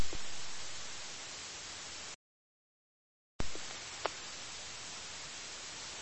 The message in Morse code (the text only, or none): none